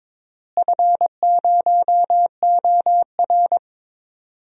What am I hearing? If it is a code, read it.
Morse code, heard as a keyed tone, about 21 wpm, 700 Hz. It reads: F0OR
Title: DXCC-100 Call Signs - 4 Characters - Repeated 2x - 22wpm